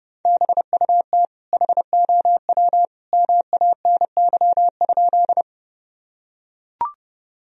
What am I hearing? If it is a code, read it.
Morse code, heard as a keyed tone, about 32 wpm, 700 Hz. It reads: BUT HOW MANY?
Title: Sentences from Top 1000 Words - Review Part 1 - 30wpm